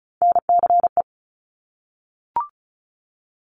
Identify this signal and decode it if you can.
Morse code, heard as a keyed tone, about 36 wpm, 700 Hz. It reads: NCE
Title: Top Three Letter Combinations in Words 35wpm